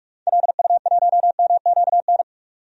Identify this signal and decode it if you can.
Morse code, heard as a keyed tone, about 47 wpm, 700 Hz. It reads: LU1MQN
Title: DXCC-100 Call Signs - 6 Characters - 45wpm